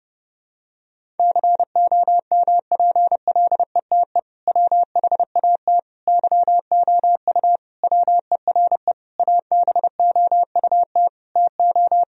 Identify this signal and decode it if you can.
Morse code, heard as a keyed tone, about 32 wpm, 700 Hz. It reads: COMPLETE WHAT YOU WERE ABOUT TO
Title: Sentences from Top 1000 Words - Review Part 2 - 30wpm